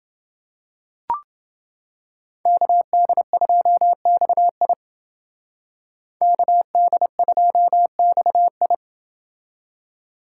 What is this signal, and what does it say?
Morse code, heard as a keyed tone, about 30 wpm, 700 Hz. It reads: KD2XI KD2XI